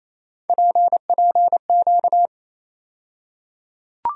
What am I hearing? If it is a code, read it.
Morse code, heard as a keyed tone, about 27 wpm, 700 Hz. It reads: PPQ
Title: USA Call Sign Postfixes - 3 Letters 28wpm